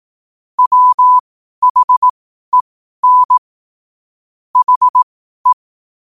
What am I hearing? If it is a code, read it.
Morse code, heard as a keyed tone, about 18 wpm, spaced as for 12 wpm, 1000 Hz. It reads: WHEN HE
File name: qotdfiles / 12wpm